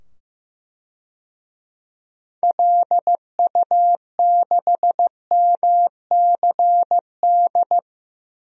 Morse code, heard as a keyed tone, about 16 wpm, 700 Hz. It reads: LU6MCD
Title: DXCC-100 Call Signs - 6 Characters - 15wpm